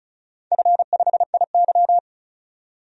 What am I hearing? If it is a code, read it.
Morse code, heard as a keyed tone, about 36 wpm, 700 Hz. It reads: F5IY
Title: DXCC-100 Call Signs - 4 Characters - 35wpm